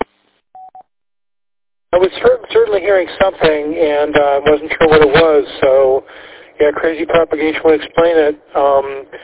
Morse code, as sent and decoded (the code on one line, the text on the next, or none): none